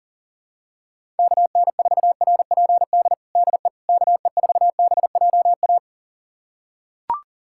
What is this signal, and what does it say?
Morse code, heard as a keyed tone, about 40 wpm, 700 Hz. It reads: KN4RPD DE KE4BJA
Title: USA Callsign DE Callsign - 6 Characters - 40wpm